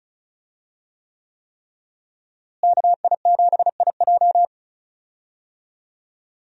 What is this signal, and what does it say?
Morse code, heard as a keyed tone, about 36 wpm, 700 Hz. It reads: KI7IJ
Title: USA Call Signs - 5 Characters - Repeated 3x - 35wpm